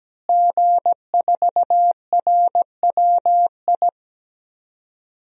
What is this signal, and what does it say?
Morse code, heard as a keyed tone, about 17 wpm, 700 Hz. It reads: G4RWI
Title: DXCC-100 Call Signs - 5 Characters - Repeated 2x - 17wpm